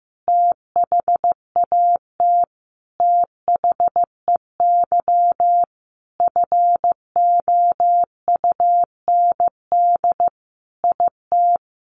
Morse code, heard as a keyed tone, about 15 wpm, 700 Hz. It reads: THAT THEY FOUND IT
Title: Sentences from Top 1100 Words 15wpm